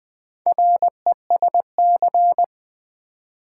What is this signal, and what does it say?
Morse code, heard as a keyed tone, about 20 wpm, 690 Hz. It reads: RESC
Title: Build A Word - 2 Letters At A Time - 6-Letter Words - 20wpm